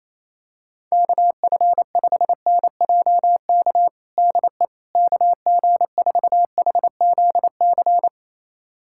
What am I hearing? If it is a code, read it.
Morse code, heard as a keyed tone, about 28 wpm, 700 Hz. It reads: KF5NJK DE KG4HZC
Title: USA Callsign DE Callsign - 6 Characters - 28wpm